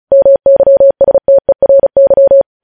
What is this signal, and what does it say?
Morse code, heard as a keyed tone, about 36 wpm, 550 Hz. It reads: MYSTERY
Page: News headlines in Morse code at 35 wpm